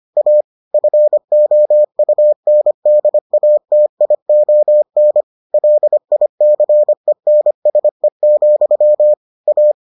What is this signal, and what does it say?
Morse code, heard as a keyed tone, about 25 wpm, 600 Hz. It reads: A FOUNDATION LICENSE, A